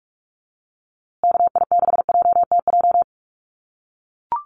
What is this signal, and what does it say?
Morse code, heard as a keyed tone, about 44 wpm, 700 Hz. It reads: KI6JTJ
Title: USA Call Signs Repeated 2x 45wpm